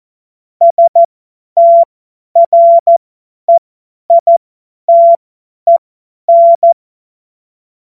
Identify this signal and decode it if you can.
Morse code, heard as a keyed tone, about 14 wpm, spaced as for 10 wpm, 680 Hz. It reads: STREITEN
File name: Morsekurs / Witze / Witz 06. Juli 2021, 50 F70